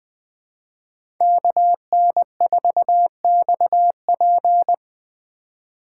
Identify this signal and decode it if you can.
Morse code, heard as a keyed tone, about 20 wpm, 700 Hz. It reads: KN4XP
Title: 2023 POTA Hunter Call Signs - 20wpm